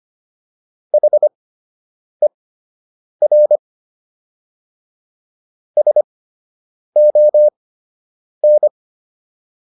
Morse code, heard as a keyed tone, about 25 wpm, spaced as for 8 wpm, 600 Hz. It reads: HER SON